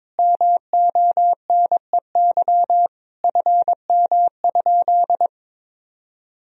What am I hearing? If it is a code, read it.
Morse code, heard as a keyed tone, about 22 wpm, 700 Hz. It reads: MONEY FM?
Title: Sentences from Top 400 Words and Abbreviations 22wpm